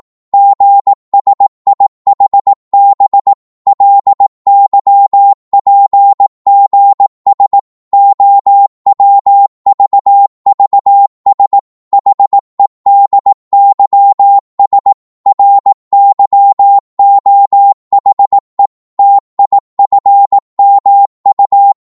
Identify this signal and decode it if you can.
Morse code, heard as a keyed tone, about 18 wpm, 800 Hz. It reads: GSIHB LYPGS OWVVS HEDYS RYOHE TIFMU
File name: Morse / ltr-18wpm